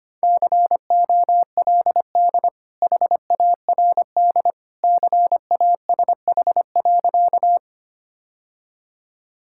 Morse code, heard as a keyed tone, about 25 wpm, 700 Hz. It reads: COLD HARD CASH.
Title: Sentences from Top 1000 Words - Review Part 5 - 25wpm